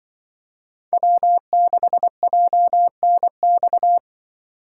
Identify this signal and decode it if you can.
Morse code, heard as a keyed tone, about 25 wpm, 700 Hz. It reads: W6JNX
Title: Speed Racing - USA Call Signs 20wpm